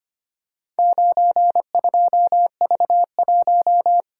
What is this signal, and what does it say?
Morse code, heard as a keyed tone, about 25 wpm, 700 Hz. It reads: 92V1
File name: practice / 25wpm / Four Letter-Number Combinations - 25wpm